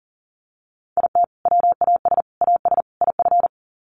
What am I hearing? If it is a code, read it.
Morse code, heard as a keyed tone, about 40 wpm, 700 Hz. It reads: IT WAS AS IF
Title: Sets of 4 Words 40wpm